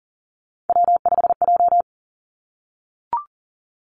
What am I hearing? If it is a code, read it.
Morse code, heard as a keyed tone, about 40 wpm, 700 Hz. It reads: W5J
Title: Three Challenging Letter-Number Combinations 40wpm